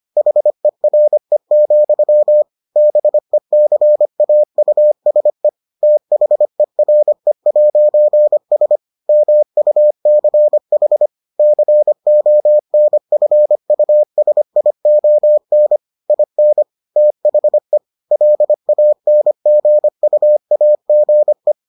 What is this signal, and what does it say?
Morse code, heard as a keyed tone, about 25 wpm, 600 Hz. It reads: HERE, BECAUSE THERE'S MUCH CONFUSION IN THE LANGUAGE